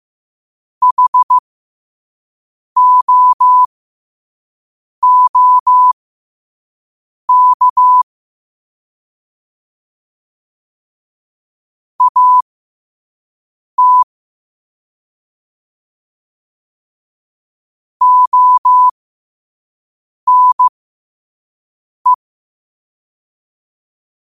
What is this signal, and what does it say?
Morse code, heard as a keyed tone, about 15 wpm, spaced as for 5 wpm, 1000 Hz. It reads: HOOK AT ONE